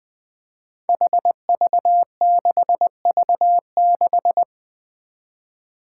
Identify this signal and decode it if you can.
Morse code, heard as a keyed tone, about 20 wpm, 700 Hz. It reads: HV6V6
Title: Five Challenging Letter-Number Combinations 20wpm